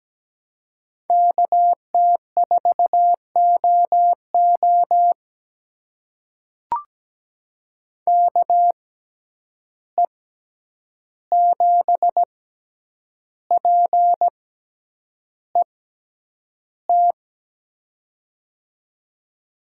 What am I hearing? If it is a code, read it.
Morse code, heard as a keyed tone, about 17 wpm, spaced as for 8 wpm, 700 Hz. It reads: KT4OO KE7PET
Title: USA Call Signs - 6x Character Spacing - 17wpm